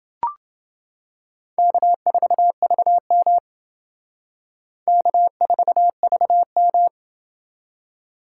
Morse code, heard as a keyed tone, about 29 wpm, 700 Hz. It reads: K4VM K4VM